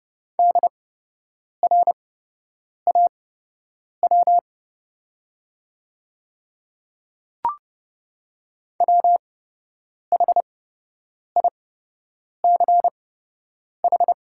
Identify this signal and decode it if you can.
Morse code, heard as a keyed tone, about 30 wpm, spaced as for 8 wpm, 700 Hz. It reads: DRAW WHICH
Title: 8x Character Spacing - Top 300 Words - 30wpm